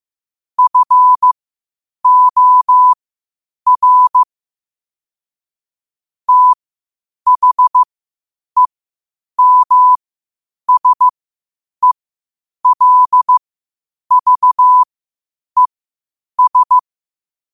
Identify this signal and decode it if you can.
Morse code, heard as a keyed tone, about 15 wpm, spaced as for 8 wpm, 1000 Hz. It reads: FOR THEMSELVES
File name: qotdfiles / 8wpm